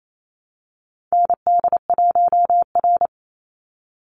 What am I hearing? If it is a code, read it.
Morse code, heard as a keyed tone, about 28 wpm, 700 Hz. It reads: ND1R
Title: Call Signs - 28wpm